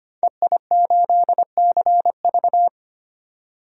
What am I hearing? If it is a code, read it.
Morse code, heard as a keyed tone, about 25 wpm, 700 Hz. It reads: EI8CV
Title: DXCC-100 Call Signs - 5 Characters - 25wpm